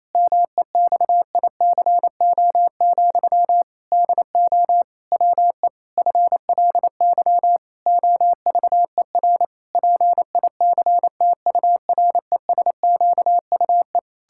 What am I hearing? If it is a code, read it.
Morse code, heard as a keyed tone, about 28 wpm, 700 Hz. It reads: MEXICO, DO WE FLY OVER PICTURESQUE
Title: Pangram Challenge v2 - 28wpm